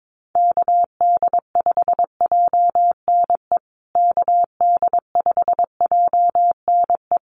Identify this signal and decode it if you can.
Morse code, heard as a keyed tone, about 22 wpm, 700 Hz. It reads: KD5JNE KD5JNE
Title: USA Call Signs Repeated 3x 22wpm